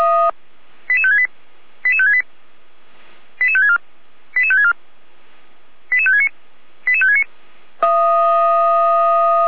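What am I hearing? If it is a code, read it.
Five-tone selective call (ZVEI), heard as quick runs of five tones, tones of about 80 ms. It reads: 88568 88568 88564 88564 88569 88569
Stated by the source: ZVEI five-tone call 88103 received on an SDR, from the Feuerwehr Orlamünde fire brigade server.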